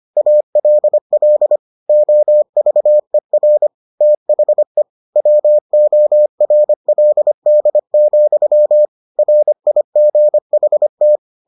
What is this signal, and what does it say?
Morse code, heard as a keyed tone, about 25 wpm, 600 Hz. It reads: ALL OVER THE WORLD, RIGHT